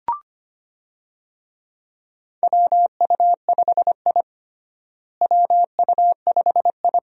Morse code, heard as a keyed tone, about 25 wpm, 700 Hz. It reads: WU5I WU5I
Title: USA Call Signs - 4 Characters - Repeated 2x - 25wpm